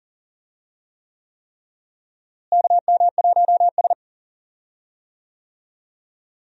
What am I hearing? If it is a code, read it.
Morse code, heard as a keyed tone, about 40 wpm, 700 Hz. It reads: KM1S
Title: USA Call Signs - 4 Characters - Repeated 3x - 40wpm